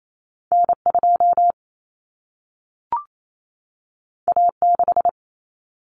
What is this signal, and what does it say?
Morse code, heard as a keyed tone, about 28 wpm, 700 Hz. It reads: N2 A6
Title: Common Cut Numbers - 2-Digit 28wpm